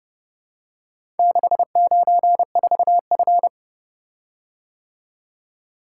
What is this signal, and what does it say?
Morse code, heard as a keyed tone, about 30 wpm, 700 Hz. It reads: 694F